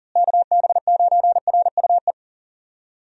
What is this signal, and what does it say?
Morse code, heard as a keyed tone, about 37 wpm, 700 Hz. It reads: KB9RUE
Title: USA Call Signs - 6 Characters - Repeated 3x - 40wpm